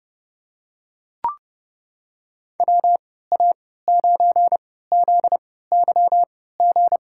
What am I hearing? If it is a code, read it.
Morse code, heard as a keyed tone, about 30 wpm, spaced as for 17 wpm, 700 Hz. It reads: WA9ZYG
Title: USA Call Signs - 6 Characters - 3x Character Spacing - 30wpm